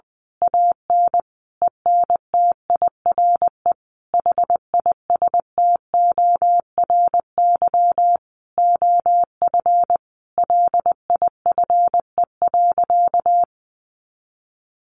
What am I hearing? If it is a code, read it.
Morse code, heard as a keyed tone, about 20 wpm, 700 Hz. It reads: AN ENTIRE HISTORY OF LIFE.
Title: A Sound Of Thunder 20wpm